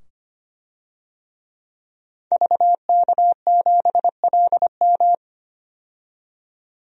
Morse code, heard as a keyed tone, about 25 wpm, 700 Hz. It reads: VK7LM
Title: DXCC-100 Call Signs - 5 Characters - 25wpm